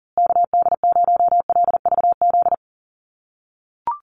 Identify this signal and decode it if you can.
Morse code, heard as a keyed tone, about 40 wpm, 700 Hz. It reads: KD0LVZ
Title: USA Call Signs Repeated 3x 40wpm